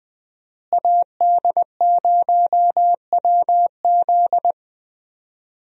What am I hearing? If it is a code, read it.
Morse code, heard as a keyed tone, about 20 wpm, 700 Hz. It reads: AD0WZ
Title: USA Call Signs - 5 Characters - Repeated 2x - 20wpm